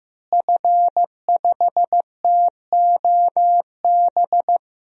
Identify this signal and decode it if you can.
Morse code, heard as a keyed tone, about 16 wpm, 700 Hz. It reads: F5TOB
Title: DXCC-100 Call Signs - 5 Characters - Repeated 3x - 15wpm